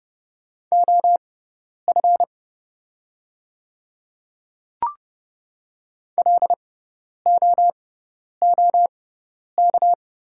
Morse code, heard as a keyed tone, about 29 wpm, spaced as for 10 wpm, 700 Hz. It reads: OF LOOK
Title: 6x Character Spacing - Top 100 Words - 30wpm